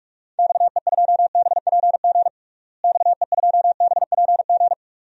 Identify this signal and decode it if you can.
Morse code, heard as a keyed tone, about 46 wpm, 700 Hz. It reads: XE2BPG XE2BPG
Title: DXCC-100 Call Signs - 6 Characters - Repeated 3x - 45wpm